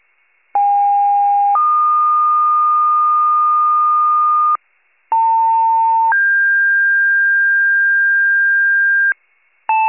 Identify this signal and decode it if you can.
Two-tone page, heard as a one-second tone, then a three-second tone, about 810 then 1220 Hz, about 900 then 1600 Hz, with noise. It is heard from 0.5 to 9.1 s.
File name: wcalert